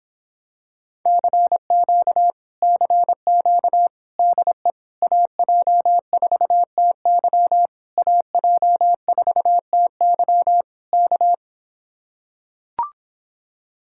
Morse code, heard as a keyed tone, about 26 wpm, 700 Hz. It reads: CQ CQ DE AJ4TY AJ4TY K